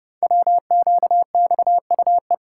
Morse code, heard as a keyed tone, about 30 wpm, 700 Hz. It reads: WQXUE
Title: Code Groups - 5 Letters - 30wpm